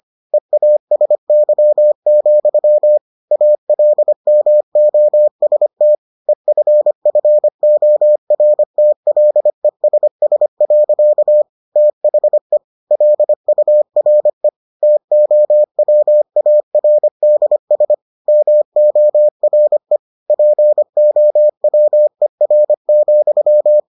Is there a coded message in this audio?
Yes: Morse code from 0.3 to 23.8 s, heard as a keyed tone, about 25 wpm, 600 Hz. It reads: EASY, ALMOST EFFORTLESS. THE LURE TOWARDS MORE POWER,